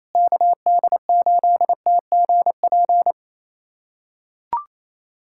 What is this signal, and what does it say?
Morse code, heard as a keyed tone, about 27 wpm, 700 Hz. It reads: KD8TGP
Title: USA Callsign DE Callsign - 6 Characters - 28wpm